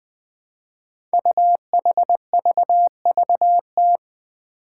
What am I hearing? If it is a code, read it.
Morse code, heard as a keyed tone, about 20 wpm, 700 Hz. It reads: UHVVT
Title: Five Challenging Letter-Number Combinations 20wpm